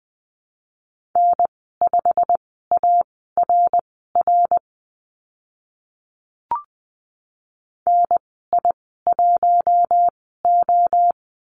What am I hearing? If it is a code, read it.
Morse code, heard as a keyed tone, about 20 wpm, spaced as for 15 wpm, 700 Hz. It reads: N5ARR NI1O
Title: 088 Lesson 15g - Number 5 - Call Signs 20wpm